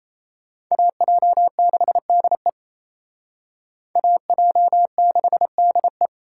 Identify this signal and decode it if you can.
Morse code, heard as a keyed tone, about 34 wpm, 700 Hz. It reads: AJ6DE AJ6DE
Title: Speed Racing - USA Call Signs 28wpm